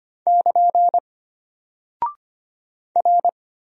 Morse code, heard as a keyed tone, about 25 wpm, 700 Hz. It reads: <KN> R
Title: Single Character (Letter:Number:Punctuation:Prosign) Alt 25wpm